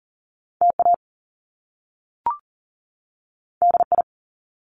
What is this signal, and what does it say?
Morse code, heard as a keyed tone, about 38 wpm, 700 Hz. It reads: TA DI
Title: Top Two Letter Combinations in Words 40wpm